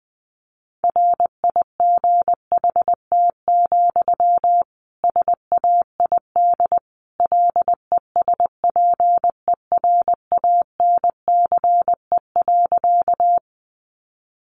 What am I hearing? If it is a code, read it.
Morse code, heard as a keyed tone, about 20 wpm, 700 Hz. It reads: RIGHT, SAID LESPERANCE.
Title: A Sound Of Thunder 20wpm